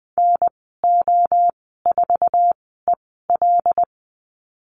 Morse code, heard as a keyed tone, about 20 wpm, spaced as for 15 wpm, 700 Hz. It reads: NO4EL
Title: 147 Lesson 24g - Letter y - Call Signs 20wpm